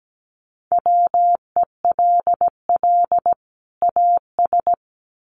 Morse code, heard as a keyed tone, about 17 wpm, 700 Hz. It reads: WELL AS
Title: Sets of 2 Words 17wpm